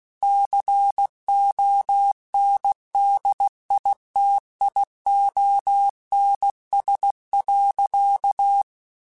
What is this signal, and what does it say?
Morse code, heard as a keyed tone, about 16 wpm, 800 Hz. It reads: CONDITIONS.
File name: Technical Subjects / Technical Subjects-mp3s / AND Gate @15